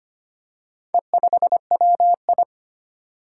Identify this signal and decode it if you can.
Morse code, heard as a keyed tone, about 25 wpm, 700 Hz. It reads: E5WI
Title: Four Challenging Letter-Number Combinations 25wpm